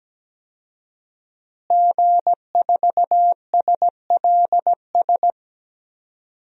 Morse code, heard as a keyed tone, about 17 wpm, 700 Hz. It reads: G4SLS
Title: DXCC-100 Call Signs - 5 Characters - Repeated 3x - 17wpm